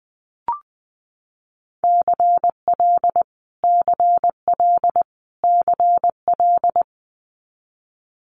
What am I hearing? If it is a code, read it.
Morse code, heard as a keyed tone, about 20 wpm, 700 Hz. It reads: CL CL CL
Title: Multi-letter Phonemes Repeated 3x 20wpm